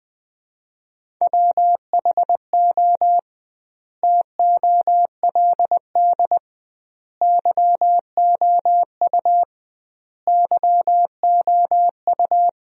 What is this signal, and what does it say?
Morse code, heard as a keyed tone, about 21 wpm, 700 Hz. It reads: WHO TOLD YOU YOU
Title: 2x Word-Spacing - Sentences from Top 400 Words - 20wpm